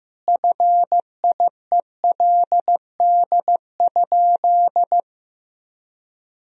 Morse code, heard as a keyed tone, about 15 wpm, 700 Hz. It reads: FIELD?
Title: Sentences from Top 1000 Words - Review Part 2 - 15wpm